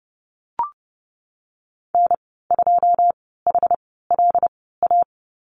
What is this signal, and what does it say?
Morse code, heard as a keyed tone, about 30 wpm, spaced as for 17 wpm, 700 Hz. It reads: N2HLA
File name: Lesson 19g - Letter w - Call Signs 30wpm